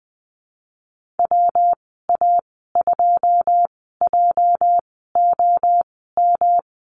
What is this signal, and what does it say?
Morse code, heard as a keyed tone, about 19 wpm, spaced as for 14 wpm, 700 Hz. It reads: WA2JOM
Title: 245 Lesson 39g - Letter z - Call Signs 20wpm